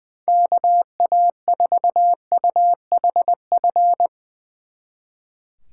Morse code, heard as a keyed tone, about 22 wpm, 700 Hz. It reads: KA4UHF
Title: Speed Racing - USA 2x3 Call Signs 20wpm